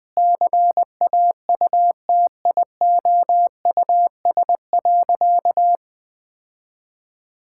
Morse code, heard as a keyed tone, about 20 wpm, 700 Hz. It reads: CAUTIOUS.
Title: Sentences with 8-Word Phrases - 20wpm